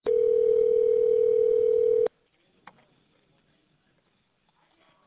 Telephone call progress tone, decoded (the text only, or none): ringback tone